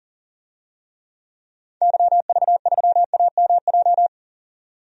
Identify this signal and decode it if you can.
Morse code, heard as a keyed tone, about 40 wpm, 700 Hz. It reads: YV3AMJ
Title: DXCC-100 Call Signs - 6 Characters - 40wpm